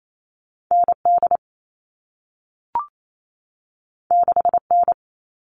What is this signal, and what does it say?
Morse code, heard as a keyed tone, about 28 wpm, 700 Hz. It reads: ND 6N
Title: All Cut Numbers - 2-Digit 28wpm